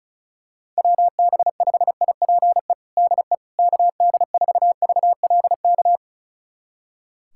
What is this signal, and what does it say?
Morse code, heard as a keyed tone, about 36 wpm, 700 Hz. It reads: WB5IPE DE KD4VLK
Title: USA Callsign DE Callsign 35wpm